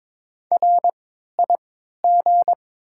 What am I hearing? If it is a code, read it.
Morse code, heard as a keyed tone, about 22 wpm, spaced as for 13 wpm, 700 Hz. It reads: RIG
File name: Top 20 QSO Elements - 3x Character Spacing - 22wpm